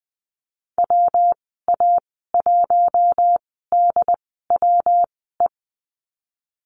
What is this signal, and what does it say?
Morse code, heard as a keyed tone, about 19 wpm, spaced as for 14 wpm, 700 Hz. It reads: WA1DWE
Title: 179 Lesson 29g - Slash - Call Signs 20wpm